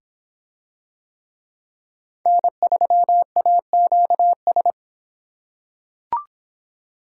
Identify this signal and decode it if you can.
Morse code, heard as a keyed tone, about 24 wpm, 700 Hz. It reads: N3AQS